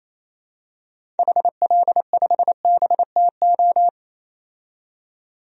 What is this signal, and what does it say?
Morse code, heard as a keyed tone, about 28 wpm, 700 Hz. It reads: HL5BTO
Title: DXCC-100 Call Signs - 6 Characters - 28wpm